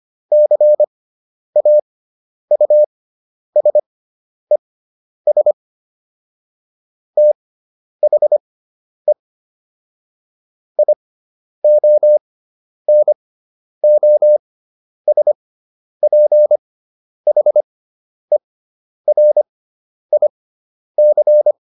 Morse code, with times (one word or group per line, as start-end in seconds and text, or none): 0.3-5.5 CAUSES
7.2-9.1 THE
10.8-21.5 IONOSPHERIC